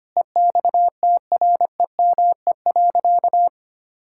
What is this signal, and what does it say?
Morse code, heard as a keyed tone, about 25 wpm, 700 Hz. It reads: EXTREME.
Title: Sentences from Top 1600 Words 25wpm